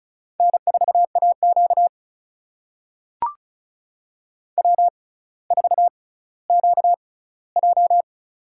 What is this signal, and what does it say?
Morse code, heard as a keyed tone, about 34 wpm, spaced as for 18 wpm, 700 Hz. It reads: N4AQ W4QJ
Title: USA Call Signs - 4 Characters - 6x Character Spacing - 35wpm